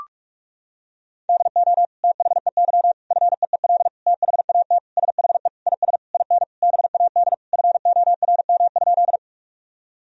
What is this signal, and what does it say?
Morse code, heard as a keyed tone, about 41 wpm, 700 Hz. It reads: DO THEY FEEL THAT SHE IS IN BAD FORM?
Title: Sentences from Top 1000 Words - Review Part 1 - 45wpm